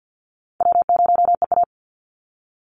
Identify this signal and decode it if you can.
Morse code, heard as a keyed tone, about 50 wpm, 700 Hz. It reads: W0EA
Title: Easy Call Signs - 50wpm